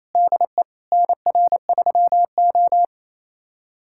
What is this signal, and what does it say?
Morse code, heard as a keyed tone, about 27 wpm, 700 Hz. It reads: DE NR3O
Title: USA Callsign DE Callsign - 4 Characters - 28wpm